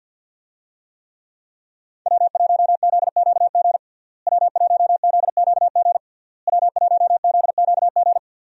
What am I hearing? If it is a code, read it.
Morse code, heard as a keyed tone, about 52 wpm, 700 Hz. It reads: W1ZQG W1ZQG W1ZQG
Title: USA Call Signs - 5 Characters - Repeated 3x - 50wpm